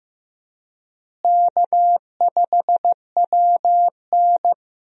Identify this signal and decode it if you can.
Morse code, heard as a keyed tone, about 15 wpm, 700 Hz. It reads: K5WN
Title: USA Call Signs - 4 Characters - Repeated 3x - 15wpm